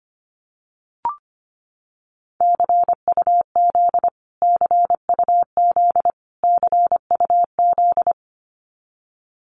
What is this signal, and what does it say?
Morse code, heard as a keyed tone, about 25 wpm, 700 Hz. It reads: CUZ CUZ CUZ
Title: Abbreviations Repeated 3x - 25wpm